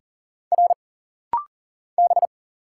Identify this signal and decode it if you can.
Morse code, heard as a keyed tone, about 40 wpm, 700 Hz. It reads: R B
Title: Single Letters - Rapid-Fire 40wpm